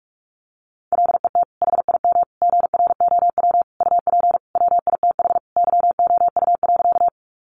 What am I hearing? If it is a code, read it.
Morse code, heard as a keyed tone, about 44 wpm, 700 Hz. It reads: LET HIM GROW UP WITH YOU.
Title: Sentences from Top 300 Words 45wpm